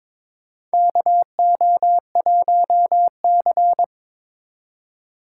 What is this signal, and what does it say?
Morse code, heard as a keyed tone, about 22 wpm, 700 Hz. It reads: KO1C
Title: USA Call Signs - 4 Characters - 3x Character Spacing - 22wpm